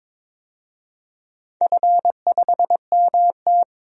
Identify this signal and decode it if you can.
Morse code, heard as a keyed tone, about 22 wpm, 700 Hz. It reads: F5MT
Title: DXCC-100 Call Signs - 4 Characters - 22wpm